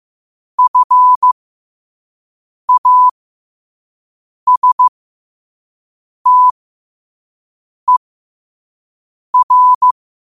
Morse code, heard as a keyed tone, about 15 wpm, spaced as for 5 wpm, 1000 Hz. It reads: FASTER
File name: qotdfiles / 5wpm